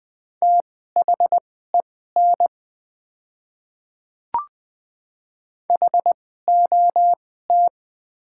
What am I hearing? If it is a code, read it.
Morse code, heard as a keyed tone, about 20 wpm, spaced as for 14 wpm, 700 Hz. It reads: THEN HOT
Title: 2x Character Spacing - Top 300 Words - 20wpm